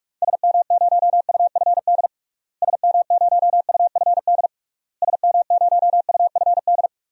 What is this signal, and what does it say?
Morse code, heard as a keyed tone, about 43 wpm, 700 Hz. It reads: SM0UFD SM0UFD SM0UFD
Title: DXCC-100 Call Signs - 6 Characters - Repeated 3x - 45wpm